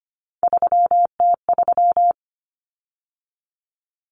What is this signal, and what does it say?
Morse code, heard as a keyed tone, about 25 wpm, 700 Hz. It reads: 3T3
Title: Common Cut Numbers - 3-Digit 25wpm